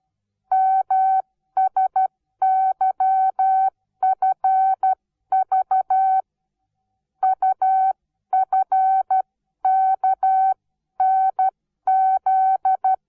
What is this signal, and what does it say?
Morse code, heard as a keyed tone, about 12 wpm, spaced as for 11 wpm, 760 Hz. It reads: MSYFV UFKNZ
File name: Onlinelehrgang / morsen / DJ4UF Morsen17